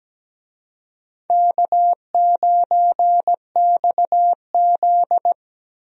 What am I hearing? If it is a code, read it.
Morse code, heard as a keyed tone, about 17 wpm, 700 Hz. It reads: K9XZ
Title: USA Call Signs - 4 Characters - Repeated 3x - 17wpm